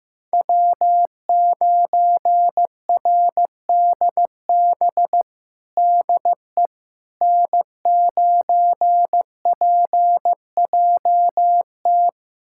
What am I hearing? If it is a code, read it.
Morse code, heard as a keyed tone, about 16 wpm, 700 Hz. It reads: W9RDB DE N9PJT
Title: USA Callsign DE Callsign - 5 Characters - 15wpm